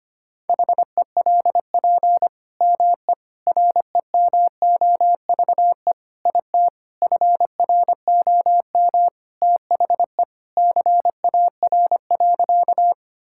Morse code, heard as a keyed tone, about 25 wpm, 700 Hz. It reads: HELP ME REMOVE IT FROM THE CAR.
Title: Sentences from Top 1000 Words - Review Part 5 - 25wpm